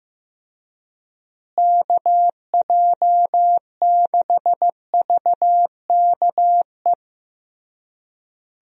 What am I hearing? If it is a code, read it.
Morse code, heard as a keyed tone, about 16 wpm, 700 Hz. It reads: KJ6VKE
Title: USA Call Signs - 6 Characters - Repeated 3x - 15wpm